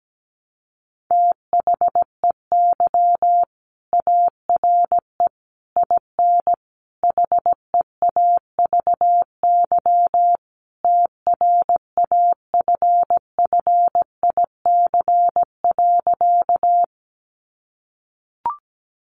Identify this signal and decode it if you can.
Morse code, heard as a keyed tone, about 17 wpm, 700 Hz. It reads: THEY ARE IN HEAVY TRAFFIC.
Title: Sentences from Top 1300 Words 17wpm